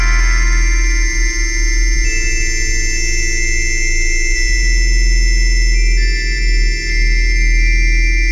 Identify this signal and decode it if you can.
Dial-up modem handshake.